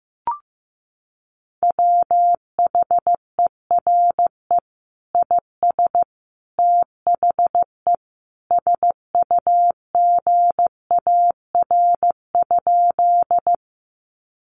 Morse code, heard as a keyed tone, about 15 wpm, 700 Hz. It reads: WHERE IS THE SUGAR?
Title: Sentences from Top 1200 Words 15wpm